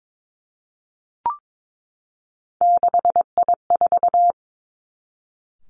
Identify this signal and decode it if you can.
Morse code, heard as a keyed tone, about 22 wpm, 700 Hz. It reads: 6I4
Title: Three Challenging Letter-Number Combinations 22wpm